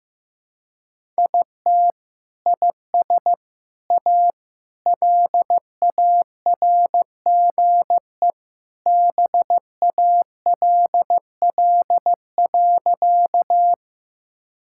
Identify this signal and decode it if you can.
Morse code, heard as a keyed tone, about 15 wpm, 700 Hz. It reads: IT IS A LARGE BALL.
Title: Sentences from Top 1000 Words - Review Part 6 - 15wpm